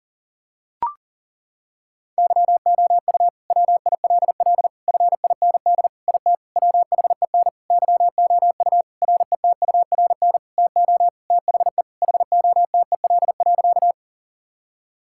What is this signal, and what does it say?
Morse code, heard as a keyed tone, about 40 wpm, 700 Hz. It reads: YOU WILL FIND IT WHEN YOU RETURN TO THE HOTEL.
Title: Sentences from Top 1600 Words 40wpm